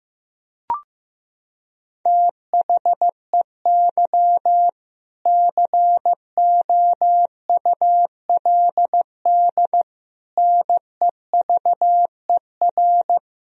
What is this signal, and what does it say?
Morse code, heard as a keyed tone, about 16 wpm, 700 Hz. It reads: THEY COULD NEVER
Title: Sentences from Top 1000 Words - Review Part 4 - 15wpm